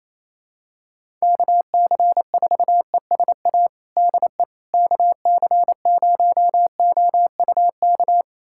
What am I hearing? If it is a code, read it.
Morse code, heard as a keyed tone, about 28 wpm, 700 Hz. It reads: KC4ESA DE KC0OUK
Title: USA Callsign DE Callsign - 6 Characters - 28wpm